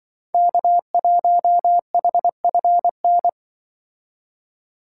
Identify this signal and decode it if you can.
Morse code, heard as a keyed tone, about 23 wpm, 700 Hz. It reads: K1HFN
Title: Speed Racing - USA 1x3 Call Signs 20wpm